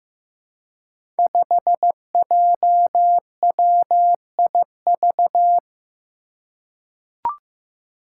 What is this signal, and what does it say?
Morse code, heard as a keyed tone, about 15 wpm, 700 Hz. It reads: 5JWIV
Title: Five Challenging Letter-Number Combinations 15wpm